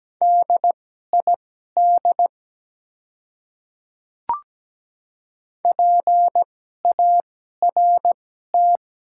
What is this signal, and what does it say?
Morse code, heard as a keyed tone, about 17 wpm, spaced as for 12 wpm, 700 Hz. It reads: DID PART
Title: 2x Character Spacing - Top 200 Words - 17wpm